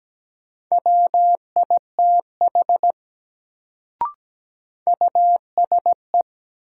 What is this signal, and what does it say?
Morse code, heard as a keyed tone, about 17 wpm, 700 Hz. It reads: WITH USE
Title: Sentences from Top 100 Words - New Word Drill - 17wpm